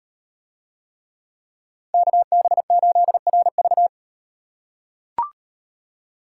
Morse code, heard as a keyed tone, about 38 wpm, 700 Hz. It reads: KB8RV